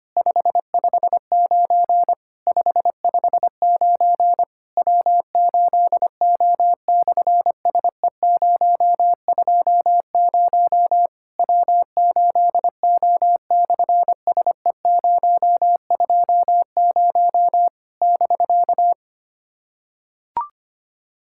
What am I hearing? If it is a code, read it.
Morse code, heard as a keyed tone, about 25 wpm, 700 Hz. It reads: 559 559 W8O/SE020 W8O/SE020 <BK>
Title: SOTA QSO Practice 25wpm